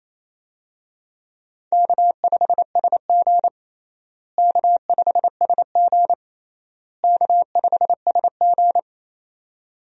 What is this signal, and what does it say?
Morse code, heard as a keyed tone, about 28 wpm, 700 Hz. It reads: K5SG K5SG K5SG
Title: USA Call Signs - 4 Characters - Repeated 3x - 28wpm